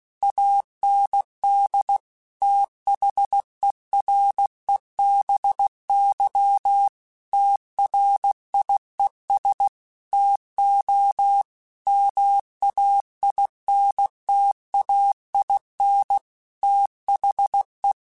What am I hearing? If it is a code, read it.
Morse code, heard as a keyed tone, about 16 wpm, 800 Hz. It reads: AND THEREBY TRIES TO MAINTAIN THE
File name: Technical Subjects / Technical Subjects-mp3s / XL @15